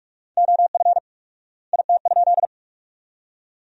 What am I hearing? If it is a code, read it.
Morse code, heard as a keyed tone, about 46 wpm, 700 Hz. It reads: OF IT?
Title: 4x Word-Spacing - Sentences from Top 100 Words - 45wpm